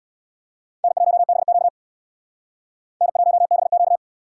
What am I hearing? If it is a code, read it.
Morse code, heard as a keyed tone, about 61 wpm, 700 Hz. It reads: N1BQ N1BQ
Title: Speed Racing - USA 1x2 Call Signs 50wpm